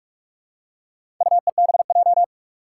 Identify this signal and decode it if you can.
Morse code, heard as a keyed tone, about 44 wpm, 700 Hz. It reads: UEBJ
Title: Four Challenging Letter-Number Combinations 45wpm